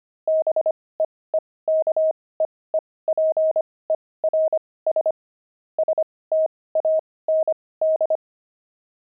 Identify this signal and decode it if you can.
Morse code, heard as a keyed tone, about 25 wpm, spaced as for 18 wpm, 630 Hz. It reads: BEEKEEPERS STAND
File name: pangram-fs-18-cs-25-beehive